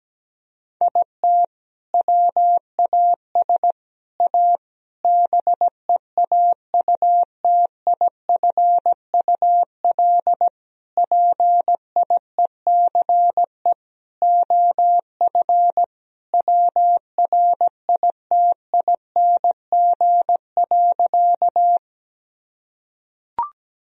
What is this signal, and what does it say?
Morse code, heard as a keyed tone, about 17 wpm, 700 Hz. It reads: IT WAS A BEAUTIFUL PIECE OF WRITING.
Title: Sentences from Top 1000 Words - Review Part 2 - 17wpm